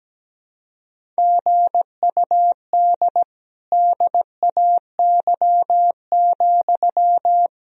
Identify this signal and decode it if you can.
Morse code, heard as a keyed tone, about 17 wpm, 700 Hz. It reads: GUD DAY,
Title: Sentences from Top 200 Words and Abbreviations 17wpm